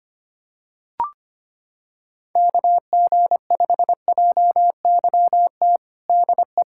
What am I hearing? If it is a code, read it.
Morse code, heard as a keyed tone, about 25 wpm, 700 Hz. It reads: KG5JYT DE
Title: USA Callsign DE Callsign 25wpm